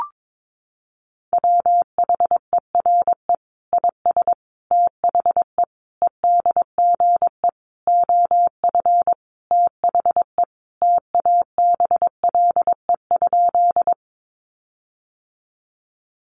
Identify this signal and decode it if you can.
Morse code, heard as a keyed tone, about 22 wpm, 700 Hz. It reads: WHERE IS THE EDGE OF THE TABLE?
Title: Sentences from Top 1300 Words 22wpm